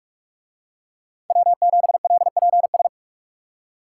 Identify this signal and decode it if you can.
Morse code, heard as a keyed tone, about 44 wpm, 700 Hz. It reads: W7LPS